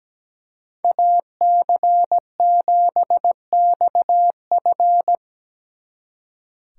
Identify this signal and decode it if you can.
Morse code, heard as a keyed tone, about 18 wpm, 700 Hz. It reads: AC7XF
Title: Speed Racing - USA Call Signs 17wpm